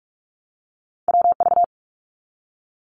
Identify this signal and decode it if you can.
Morse code, heard as a keyed tone, about 44 wpm, 700 Hz. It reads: WV
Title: US States Repeated 3x 45wpm